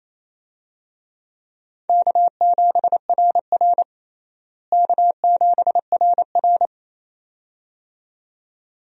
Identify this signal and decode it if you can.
Morse code, heard as a keyed tone, about 28 wpm, 700 Hz. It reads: K7RR K7RR
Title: USA Call Signs - 4 Characters - Repeated 2x - 28wpm